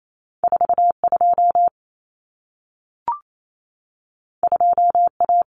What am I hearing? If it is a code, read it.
Morse code, heard as a keyed tone, about 28 wpm, 700 Hz. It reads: 42 2A
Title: Common Cut Numbers - 2-Digit 28wpm